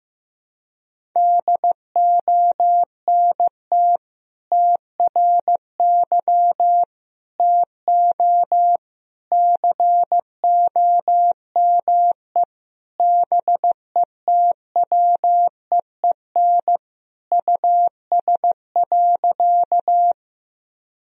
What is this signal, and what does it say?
Morse code, heard as a keyed tone, about 15 wpm, 700 Hz. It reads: DONT TRY TO COME BETWEEN US.
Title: Sentences from Top 1000 Words - Review Part 1 - 15wpm